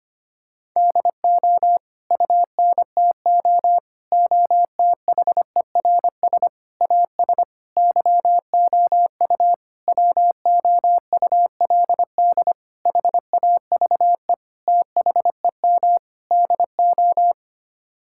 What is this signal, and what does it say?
Morse code, heard as a keyed tone, about 25 wpm, 700 Hz. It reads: DO UNTO OTHERS AS YOU WOULD HAVE THEM DO
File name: Sets of 9 Words - 25wpm